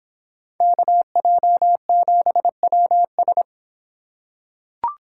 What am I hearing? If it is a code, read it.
Morse code, heard as a keyed tone, about 26 wpm, 700 Hz. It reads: KJ7WS